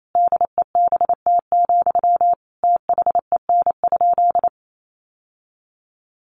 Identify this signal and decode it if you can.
Morse code, heard as a keyed tone, about 28 wpm, 700 Hz. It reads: DEBT, THEN?
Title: Sentences from Top 1300 Words 28wpm